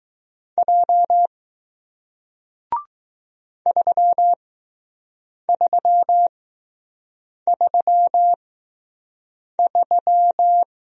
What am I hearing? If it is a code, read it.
Morse code, heard as a keyed tone, about 20 wpm, spaced as for 6 wpm, 700 Hz. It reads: J 3333